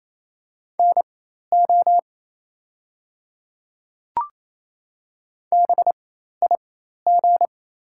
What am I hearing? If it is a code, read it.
Morse code, heard as a keyed tone, about 28 wpm, spaced as for 13 wpm, 690 Hz. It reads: NO BIG
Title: 4x Character Spacing - Top 200 Words - 28wpm